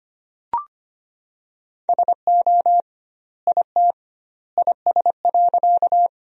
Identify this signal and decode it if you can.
Morse code, heard as a keyed tone, about 25 wpm, 700 Hz. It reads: SO IT IS.
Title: 2x Word-Spacing - Sentences from Top 100 Words - 25wpm